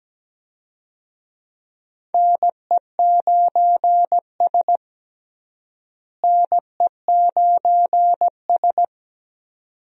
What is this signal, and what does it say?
Morse code, heard as a keyed tone, about 17 wpm, 700 Hz. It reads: NE9S NE9S
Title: USA Call Signs - 4 Characters - Repeated 2x - 17wpm